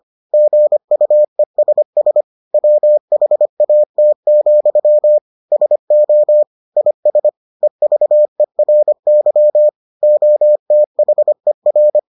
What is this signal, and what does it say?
Morse code, heard as a keyed tone, about 25 wpm, 600 Hz. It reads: GUESS WHAT, SO IS EVERY OTHER